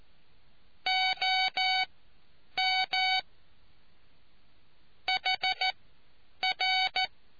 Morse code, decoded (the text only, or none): OM HR